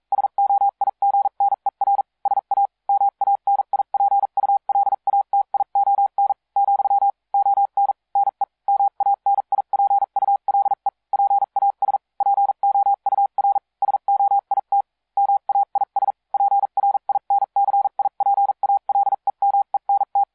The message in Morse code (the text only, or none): SOIGNER SA MANIPULATION , ON NE MANIPULE PAS POUR SOIT MAIS PRINCIPALEMENT